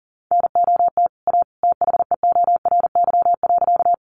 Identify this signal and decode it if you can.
Morse code, heard as a keyed tone, about 40 wpm, 700 Hz. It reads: NOT A THEORY.
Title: Sentences from Top 600 Words 40wpm